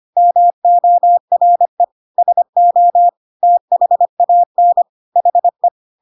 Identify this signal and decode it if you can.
Morse code, heard as a keyed tone, about 25 wpm, 700 Hz. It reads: MORE SO THAN HE
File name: Pinocchio / Pinocchio ch3 25 25 ik1hge